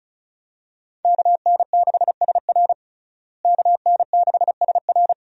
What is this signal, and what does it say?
Morse code, heard as a keyed tone, about 34 wpm, 700 Hz. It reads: KN6SR KN6SR
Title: USA Call Signs - 5 Characters - Repeated 2x - 35wpm